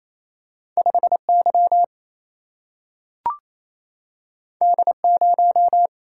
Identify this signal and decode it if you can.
Morse code, heard as a keyed tone, about 28 wpm, 700 Hz. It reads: 5 Y D 0
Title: Single-Letter Single-Number 28wpm